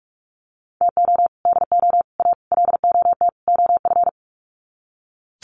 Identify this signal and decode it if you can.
Morse code, heard as a keyed tone, about 44 wpm, 700 Hz. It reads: TO DO A LOT OF
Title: Sets of 5 Words Encore 45wpm